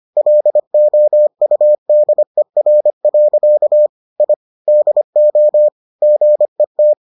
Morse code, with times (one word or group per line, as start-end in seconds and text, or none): 0.2-3.9 LOUDER.
4.2-4.3 I
4.7-5.7 DO
6.0-6.9 GET